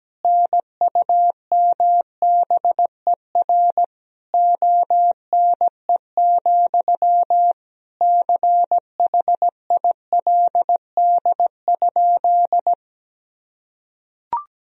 Morse code, heard as a keyed tone, about 17 wpm, 700 Hz. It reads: NUMBER ONE, CHILD?
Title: Sentences from Top 1000 Words - Review Part 1 - 17wpm